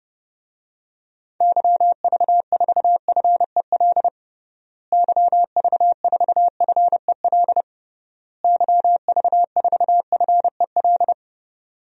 Morse code, heard as a keyed tone, about 32 wpm, 700 Hz. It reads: YV4FEL YV4FEL YV4FEL
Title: DXCC-100 Call Signs - 6 Characters - Repeated 3x - 30wpm